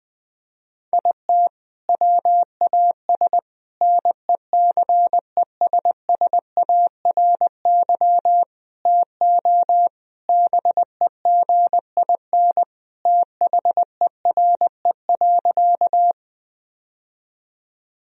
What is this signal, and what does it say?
Morse code, heard as a keyed tone, about 21 wpm, 700 Hz. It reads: IT WAS NECESSARY TO BEGIN THERE.
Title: Sentences from Top 1000 Words - Review Part 1 - 20wpm